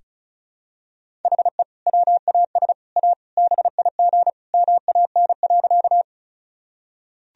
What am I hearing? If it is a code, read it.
Morse code, heard as a keyed tone, about 34 wpm, 700 Hz. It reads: HE WAS A BIG MAN.
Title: Sentences from Top 1000 Words - Review Part 1 - 35wpm